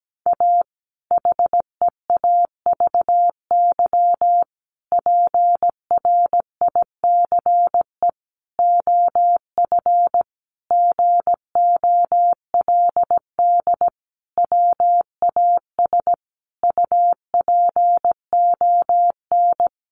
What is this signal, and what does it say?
Morse code, heard as a keyed tone, about 17 wpm, 700 Hz. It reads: A HEAVY PRICE OF GOLD WAS UPON
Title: Sentences from Top 1100 Words 17wpm